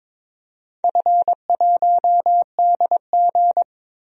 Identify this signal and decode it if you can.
Morse code, heard as a keyed tone, about 22 wpm, 700 Hz. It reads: F1DG
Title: DXCC-100 Call Signs - 4 Characters - 22wpm